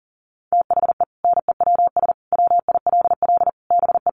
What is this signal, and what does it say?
Morse code, heard as a keyed tone, about 40 wpm, 700 Hz. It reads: THE NEWS WILL BE
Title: Sentences from Top 1200 Words 40wpm